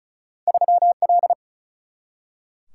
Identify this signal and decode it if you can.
Morse code, heard as a keyed tone, about 36 wpm, 700 Hz. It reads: 3 L
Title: Single-Letter Single-Number 35wpm